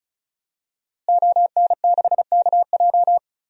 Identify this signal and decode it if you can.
Morse code, heard as a keyed tone, about 36 wpm, 700 Hz. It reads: ON6KJ
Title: DXCC-100 Call Signs - 5 Characters - Repeated 2x - 35wpm